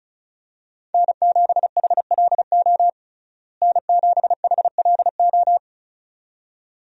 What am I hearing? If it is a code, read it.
Morse code, heard as a keyed tone, about 34 wpm, 700 Hz. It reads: N7HLO N7HLO
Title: USA Call Signs - 5 Characters - Repeated 2x - 35wpm